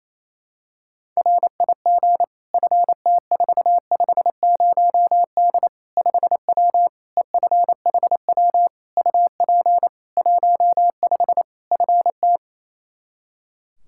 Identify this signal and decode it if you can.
Morse code, heard as a keyed tone, about 27 wpm, 700 Hz. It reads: RIG FT450D 5W EFHW UP 15 FT
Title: Practice QSO - Exchange 2 - Info - 28wpm